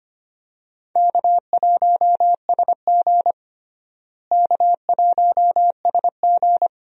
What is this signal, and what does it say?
Morse code, heard as a keyed tone, about 25 wpm, 700 Hz. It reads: K1SG K1SG
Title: USA Call Signs - 4 Characters - Repeated 2x - 25wpm